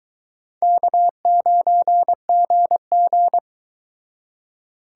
Morse code, heard as a keyed tone, about 22 wpm, 700 Hz. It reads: K9GG